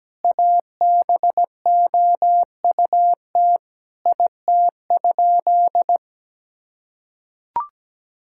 Morse code, heard as a keyed tone, about 17 wpm, 700 Hz. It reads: ABOUT IT?
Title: Sentences from Top 1000 Words - Review Part 6 - 17wpm